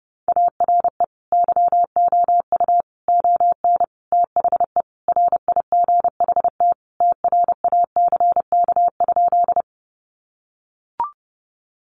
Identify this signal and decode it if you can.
Morse code, heard as a keyed tone, about 30 wpm, 700 Hz. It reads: ARE YOU ON THE RIGHT TRACK?
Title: Sentences from Top 1100 Words 30wpm